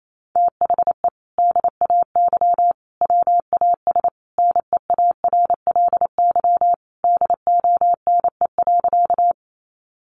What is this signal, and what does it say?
Morse code, heard as a keyed tone, about 28 wpm, 700 Hz. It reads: THE DAY WAS NEARLY DONE.
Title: Sentences from Top 1300 Words 28wpm